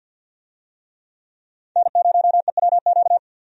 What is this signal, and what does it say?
Morse code, heard as a keyed tone, about 53 wpm, 700 Hz. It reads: N0EWQ
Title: USA Call Signs - 5 Characters - 50wpm